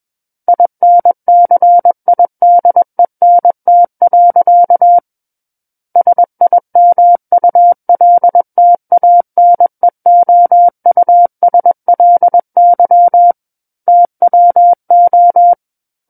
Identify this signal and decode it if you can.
Morse code, heard as a keyed tone, about 21 wpm, 700 Hz. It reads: INCIDENT. SIMULTANEOUSLY TWO
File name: Book One / 12-Chapter 12 - 21 WPM